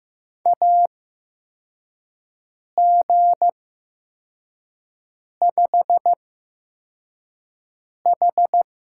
Morse code, heard as a keyed tone, about 15 wpm, spaced as for 4 wpm, 700 Hz. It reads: AG5H
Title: USA Call Signs - 4 Characters - 8x Character Spacing - 15wpm